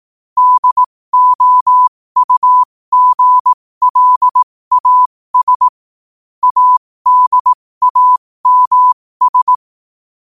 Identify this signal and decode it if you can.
Morse code, heard as a keyed tone, about 18 wpm, spaced as for 15 wpm, 1000 Hz. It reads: DOUGLAS ADAMS